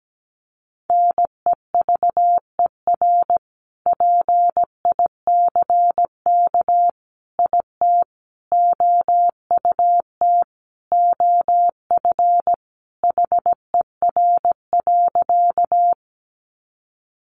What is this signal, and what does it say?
Morse code, heard as a keyed tone, about 17 wpm, 700 Hz. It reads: NEVER PICK IT OUT OF HER.
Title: Sentences from Top 1000 Words 17wpm